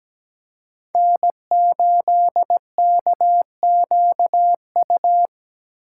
Morse code, heard as a keyed tone, about 17 wpm, 700 Hz. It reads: N8KQU
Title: USA Call Signs - 5 Characters - 6x Character Spacing - 17wpm